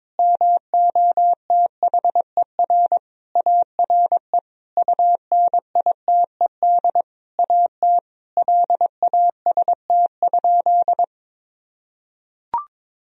Morse code, heard as a keyed tone, about 22 wpm, 700 Hz. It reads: MOTHER ARE UNITED AT LAST?
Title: Sentences from Top 1000 Words - Review Part 4 - 22wpm